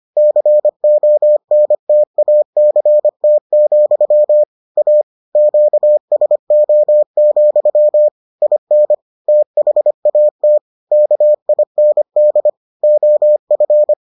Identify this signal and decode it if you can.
Morse code, heard as a keyed tone, about 25 wpm, 600 Hz. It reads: CONTACT, A QSO, IN THAT KIND OF